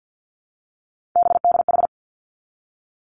Morse code, heard as a keyed tone, about 50 wpm, 700 Hz. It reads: BDH
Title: Three Challenging Letter-Number Combinations 50wpm